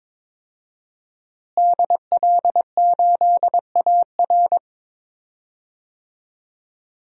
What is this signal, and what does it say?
Morse code, heard as a keyed tone, about 22 wpm, 700 Hz. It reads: DL8AR
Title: DXCC-100 Call Signs - 5 Characters - Repeated 3x - 22wpm